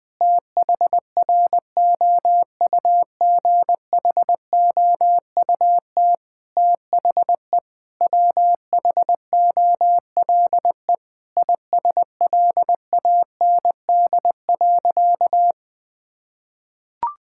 Morse code, heard as a keyed tone, about 21 wpm, 700 Hz. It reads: THROUGHOUT THE WHOLE ISLAND.
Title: Sentences from Top 1000 Words - Review Part 5 - 20wpm